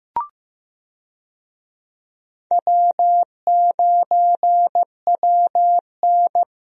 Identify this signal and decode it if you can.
Morse code, heard as a keyed tone, about 15 wpm, 700 Hz. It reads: W9WN
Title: USA Call Signs - 4 Characters - Repeated 3x - 15wpm